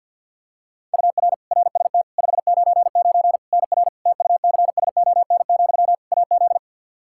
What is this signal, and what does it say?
Morse code, heard as a keyed tone, about 51 wpm, 700 Hz. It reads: UR RST 599 NR TUCSON, AZ
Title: Practice QSO - Exchange 1 - Beginning - 50wpm